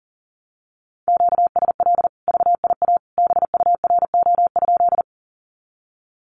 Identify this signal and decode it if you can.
Morse code, heard as a keyed tone, about 40 wpm, 700 Hz. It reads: QSL VIA BURO?
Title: Common QSO Phrases 40wpm